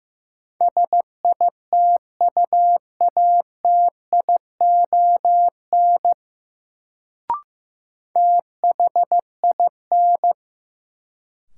Morse code, heard as a keyed tone, about 15 wpm, 700 Hz. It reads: SITUATION THIN
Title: Sentences from Top 1200 Words - New Word Drill - 15wpm